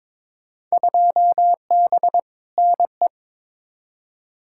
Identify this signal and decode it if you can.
Morse code, heard as a keyed tone, about 22 wpm, 700 Hz. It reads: 2B NE
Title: Field Day Exchanges Encore - 22wpm